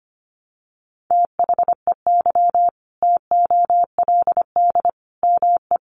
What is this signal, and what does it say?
Morse code, heard as a keyed tone, about 25 wpm, 700 Hz. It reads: THEY TOLD ME
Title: Sentences from Top 400 Words 25wpm